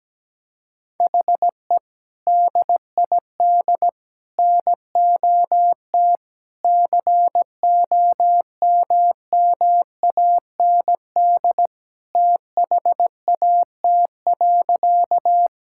Morse code, heard as a keyed tone, about 17 wpm, 700 Hz. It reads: HE DID NOT COMMAND THAT.
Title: Sentences from Top 1600 Words 17wpm